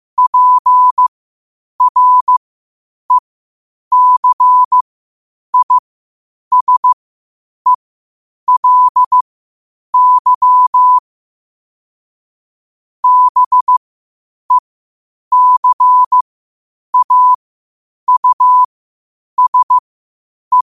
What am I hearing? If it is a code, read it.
Morse code, heard as a keyed tone, about 15 wpm, spaced as for 8 wpm, 1000 Hz. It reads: PRECISELY BECAUSE
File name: qotdfiles / 8wpm